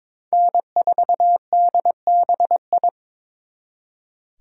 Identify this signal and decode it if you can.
Morse code, heard as a keyed tone, about 26 wpm, 700 Hz. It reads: N4DBI
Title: Speed Racing - USA 1x3 Call Signs 22wpm